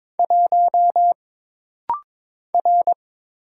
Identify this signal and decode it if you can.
Morse code, heard as a keyed tone, about 23 wpm, 700 Hz. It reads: 1 R
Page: This rapid-fire timing is geared to push your ICR (Instant Character Recognition) skills.